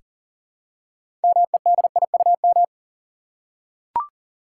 Morse code, heard as a keyed tone, about 40 wpm, 700 Hz. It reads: MEDIUM
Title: Six Letter Words - 40wpm